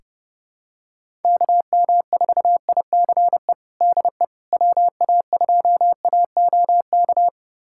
Morse code, heard as a keyed tone, about 30 wpm, 700 Hz. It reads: KM4ICE DE WA2AOK
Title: USA Callsign DE Callsign - 6 Characters - 30wpm